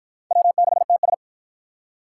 Morse code, heard as a keyed tone, about 53 wpm, 700 Hz. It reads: W6TS